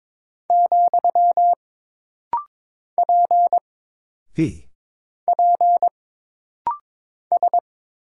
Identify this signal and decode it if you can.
Morse code, heard as a keyed tone, about 22 wpm, 700 Hz. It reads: , P P S
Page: This rapid-fire timing is geared to push your ICR (Instant Character Recognition) skills.